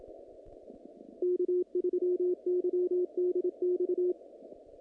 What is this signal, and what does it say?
Morse code, heard as a keyed tone, about 27 wpm, 360 Hz, with radio noise. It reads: K3YDX
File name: contests / naqp-cw-k5tr-op-k5pi- -2100